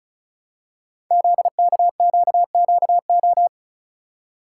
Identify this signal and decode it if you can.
Morse code, heard as a keyed tone, about 36 wpm, 700 Hz. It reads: ZKQQO
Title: Code Groups - 5 Letters - 35wpm